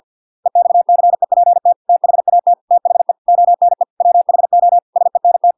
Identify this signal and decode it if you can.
Morse code, heard as a keyed tone, about 50 wpm, 700 Hz. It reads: EXCEPT THAT THE ONE WHO SENT